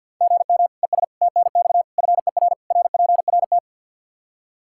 Morse code, heard as a keyed tone, about 52 wpm, 700 Hz. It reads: GM ES TNX FER RPRT
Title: Practice QSO - Exchange 1 - Beginning - 50wpm